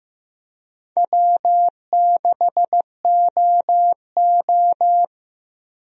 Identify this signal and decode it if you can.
Morse code, heard as a keyed tone, about 15 wpm, 700 Hz. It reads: W6OO
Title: USA Call Signs - 4 Characters - 15wpm